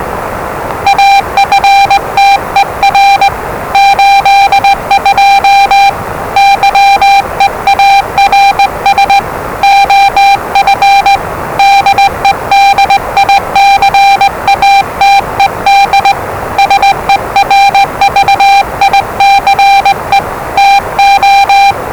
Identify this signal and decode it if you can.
Morse code, heard as a keyed tone, about 20 wpm, 790 Hz, with noise. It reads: AFTER 82 YEARS OF DEDICATED SERVICE TO
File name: WellingtonRadioZlwCloseDownOnHfAndMf / 01-Zlw 4277khz 417.5 qrt until 0925z